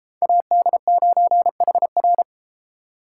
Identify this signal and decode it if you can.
Morse code, heard as a keyed tone, about 33 wpm, 700 Hz. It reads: AD9HR